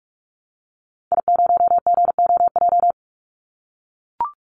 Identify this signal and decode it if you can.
Morse code, heard as a keyed tone, about 44 wpm, 700 Hz. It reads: I0GOJ
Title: Call Signs - 45wpm